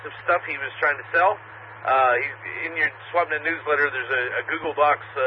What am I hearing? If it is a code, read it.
Emergency Alert System (SAME) tones, heard as harsh buzzing bursts, with speech. It is heard from 0.5 to 4.5 s.